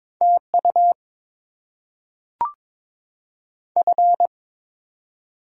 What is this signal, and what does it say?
Morse code, heard as a keyed tone, about 22 wpm, 700 Hz. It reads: T U F
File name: Top 20 QSO Elements - 8x Character Spacing - 22wpm